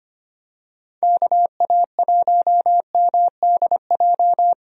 Morse code, heard as a keyed tone, about 25 wpm, 700 Hz. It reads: KA1MDJ
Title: USA Call Signs - 6 Characters - Repeated 3x - 25wpm